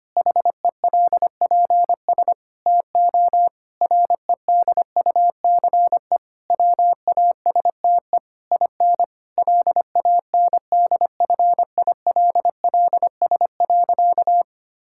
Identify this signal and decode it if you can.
Morse code, heard as a keyed tone, about 25 wpm, 700 Hz. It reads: HELPS TO REDUCE WASTE IN LANDFILLS.
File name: practice / 25wpm / Difficult Sentences with Hints - 25wpm